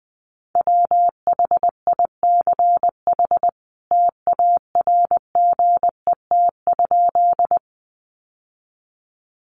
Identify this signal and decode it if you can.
Morse code, heard as a keyed tone, about 20 wpm, 700 Hz. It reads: WHICH TARGET?
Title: Sentences from Top 1200 Words 20wpm